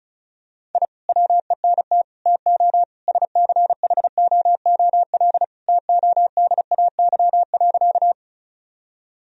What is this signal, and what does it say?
Morse code, heard as a keyed tone, about 36 wpm, 700 Hz. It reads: I WENT TO SCHOOL TODAY.
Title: Sentences from Top 1000 Words - Review Part 1 - 35wpm